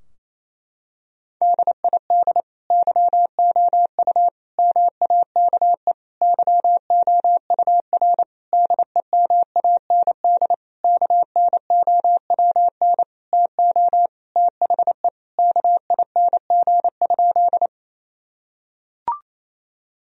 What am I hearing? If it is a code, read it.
Morse code, heard as a keyed tone, about 28 wpm, 700 Hz. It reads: DID YOU MAKE YOUR DEMAND KNOWN TO THE KING?
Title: Sentences from Top 1000 Words - Review Part 4 - 28wpm